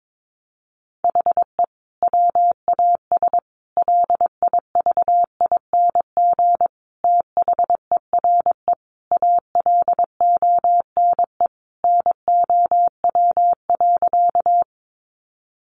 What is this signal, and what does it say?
Morse code, heard as a keyed tone, about 22 wpm, 700 Hz. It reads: HE WAS LIVING THERE ALONE NOW.
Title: Sentences from Top 1100 Words 22wpm